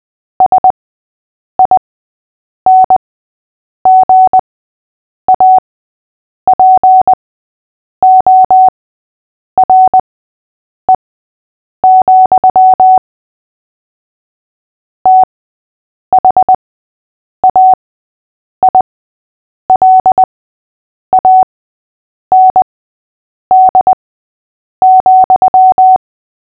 Morse code, heard as a keyed tone, about 20 wpm, spaced as for 8 wpm, 750 Hz. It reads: SINGAPORE, THAILAND,
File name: cw / fists / Short-Story-3-8wpm